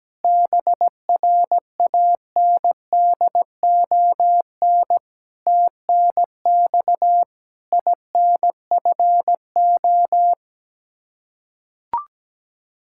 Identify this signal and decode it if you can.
Morse code, heard as a keyed tone, about 17 wpm, 700 Hz. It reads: BRANDON TNX INFO
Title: Practice QSO - Exchange 3 - More Info - 17wpm